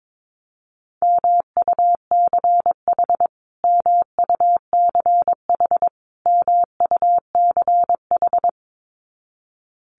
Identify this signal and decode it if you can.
Morse code, heard as a keyed tone, about 22 wpm, 700 Hz. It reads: MUCH MUCH MUCH
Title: Top 200 Words Repeated 22wpm